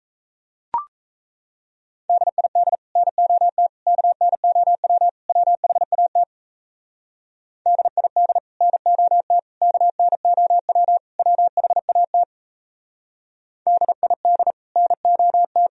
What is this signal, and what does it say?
Morse code, heard as a keyed tone, about 36 wpm, 700 Hz. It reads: DID NOT KNOW WHAT DID NOT KNOW WHAT DID NOT